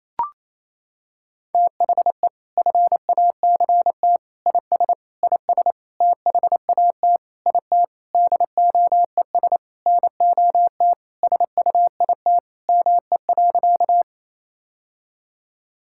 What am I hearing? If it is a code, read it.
Morse code, heard as a keyed tone, about 28 wpm, 700 Hz. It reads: THE FACT IS IS THAT IT DOES NOT SUIT ME.
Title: Sentences from Top 1600 Words 28wpm